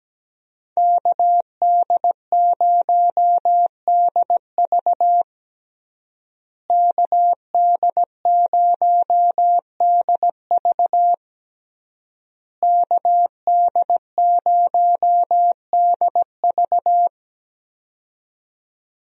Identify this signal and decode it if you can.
Morse code, heard as a keyed tone, about 17 wpm, 700 Hz. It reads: KD0DV KD0DV KD0DV
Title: USA Call Signs - 5 Characters - Repeated 3x - 17wpm